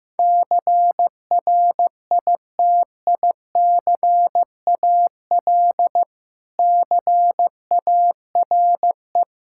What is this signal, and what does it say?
Morse code, heard as a keyed tone, about 15 wpm, 700 Hz. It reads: CRITICAL CARE
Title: Practice QSO - Exchange 3 - More Info - 15wpm